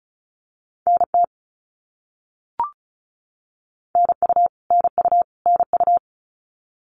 Morse code, heard as a keyed tone, about 36 wpm, 700 Hz. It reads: NT NU NU NU
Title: Canadian Provinces and Territories - Repeated 3x - 35wpm